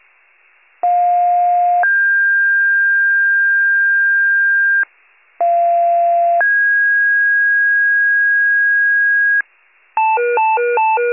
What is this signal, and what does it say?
Two-tone page, heard as a one-second tone, then a three-second tone, about 710 then 1600 Hz, about 690 then 1650 Hz. It is heard from 0.8 to 9.4 s.